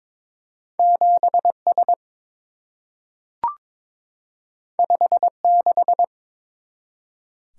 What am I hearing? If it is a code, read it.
Morse code, heard as a keyed tone, about 23 wpm, 700 Hz. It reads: 7S 56
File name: practice / 22wpm / Two Letter-Number Combinations - 22wpm